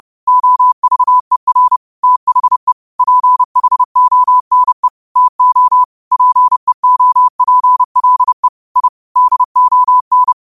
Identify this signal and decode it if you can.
Morse code, heard as a keyed tone, about 30 wpm, 1000 Hz. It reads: OVER THE PHONE TO PEOPLE I DON
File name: qotdfiles / 30wpm